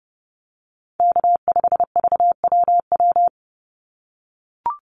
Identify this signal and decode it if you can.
Morse code, heard as a keyed tone, about 30 wpm, 700 Hz. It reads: K5VWW
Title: Call Signs - 30wpm